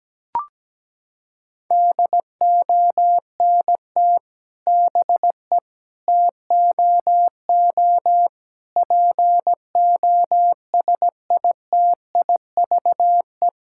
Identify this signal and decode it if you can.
Morse code, heard as a keyed tone, about 17 wpm, 700 Hz. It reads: DONT BE TOO POSITIVE
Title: Sentences from Top 1000 Words - Review Part 5 - 17wpm